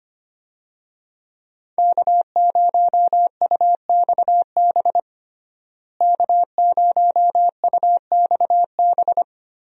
Morse code, heard as a keyed tone, about 25 wpm, 690 Hz. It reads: K0UXB K0UXB
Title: USA Call Signs - 5 Characters - Repeated 2x - 25wpm